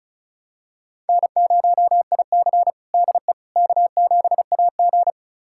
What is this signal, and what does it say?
Morse code, heard as a keyed tone, about 36 wpm, 700 Hz. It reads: N0IC DE K7AG
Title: USA Callsign DE Callsign - 4 Characters - 35wpm